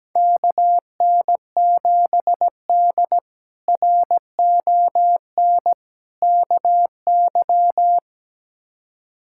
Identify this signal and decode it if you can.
Morse code, heard as a keyed tone, about 17 wpm, 700 Hz. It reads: KN7D RON KY
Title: K1USN SST QSO Practice - 17wpm